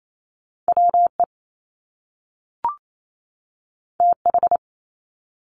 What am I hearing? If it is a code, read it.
Morse code, heard as a keyed tone, about 28 wpm, 700 Hz. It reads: WE TH
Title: Top Two Letter Combinations in Words 28wpm